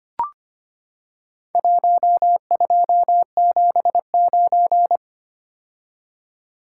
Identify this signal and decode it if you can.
Morse code, heard as a keyed tone, about 25 wpm, 700 Hz. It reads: 1279
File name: practice / 25wpm / Four Letter-Number Combinations - 25wpm